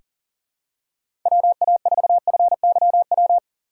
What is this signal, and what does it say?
Morse code, heard as a keyed tone, about 40 wpm, 700 Hz. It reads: WA4FYW
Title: USA Call Signs - 6 Characters - 2x Character Spacing - 40wpm